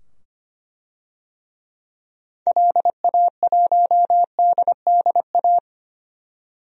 Morse code, heard as a keyed tone, about 25 wpm, 700 Hz. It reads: LA1DDA
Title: DXCC-100 Call Signs - 6 Characters - 25wpm